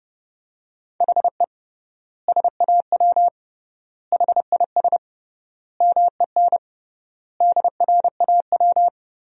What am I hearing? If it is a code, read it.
Morse code, heard as a keyed tone, about 32 wpm, 700 Hz. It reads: HE SAW HIS MEN DRAW
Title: 3x Word-Spacing - Sentences from Top 300 Words - 30wpm